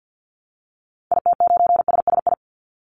Morse code, heard as a keyed tone, about 48 wpm, 700 Hz. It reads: IT9SSI
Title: Call Signs Repeated 3x - 50wpm